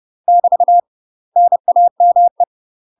Morse code, heard as a keyed tone, about 30 wpm, 700 Hz. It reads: <BT> NAME
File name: UK MASTER QSOs TRAINING / 30WPM / UK MASTER QSOs 0019